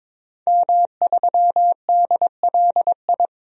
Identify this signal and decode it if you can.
Morse code, heard as a keyed tone, about 22 wpm, 700 Hz. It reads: M3DLI
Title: DXCC-100 Call Signs - 5 Characters - 22wpm